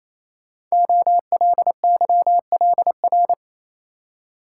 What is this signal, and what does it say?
Morse code, heard as a keyed tone, about 27 wpm, 700 Hz. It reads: OLYLR
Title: Code Groups - 5 Letter:Numbers - 28wpm